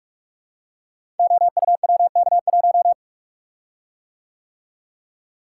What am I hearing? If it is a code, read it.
Morse code, heard as a keyed tone, about 44 wpm, 700 Hz. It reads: OUWK1
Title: Code Groups - 5 Letter:Numbers - 45wpm